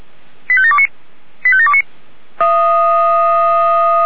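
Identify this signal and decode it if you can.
Five-tone selective call (ZVEI), heard as quick runs of five tones, tones of about 70 ms. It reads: 85729 85729
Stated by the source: ZVEI five-tone selective call received with an SDR.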